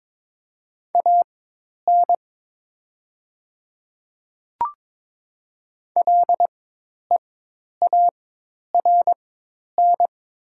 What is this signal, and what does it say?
Morse code, heard as a keyed tone, about 22 wpm, spaced as for 10 wpm, 700 Hz. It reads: AN LEARN
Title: 4x Character Spacing - Top 300 Words - 22wpm